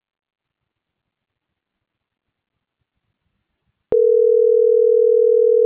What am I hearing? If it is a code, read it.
Ringback tone (North America), 440 + 480 Hz.